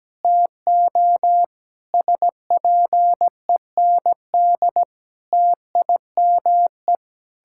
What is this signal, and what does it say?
Morse code, heard as a keyed tone, about 17 wpm, 700 Hz. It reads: TO SPEND TIME
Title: Sentences from Top 1000 Words - Review Part 6 - 17wpm